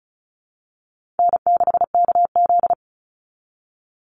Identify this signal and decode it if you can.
Morse code, heard as a keyed tone, about 34 wpm, 700 Hz. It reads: N6KZ
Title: NA SOTA Chaser Callsigns 35wpm